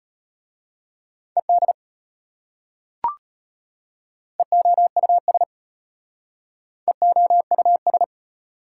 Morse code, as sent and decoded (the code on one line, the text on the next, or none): . -.. / . --- ..- ... / . --- ..- ...
ED EOUS EOUS